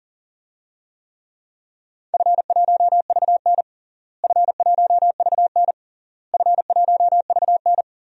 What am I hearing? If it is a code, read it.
Morse code, heard as a keyed tone, about 40 wpm, 700 Hz. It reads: F1VN F1VN F1VN
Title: DXCC-100 Call Signs - 4 Characters - Repeated 3x - 40wpm